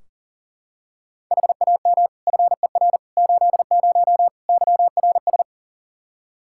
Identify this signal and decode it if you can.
Morse code, heard as a keyed tone, about 40 wpm, 700 Hz. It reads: HAM FER 80 YRS
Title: Practice QSO - Exchange 3 - More Info - 40wpm